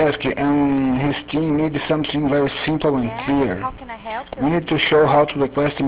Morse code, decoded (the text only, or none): none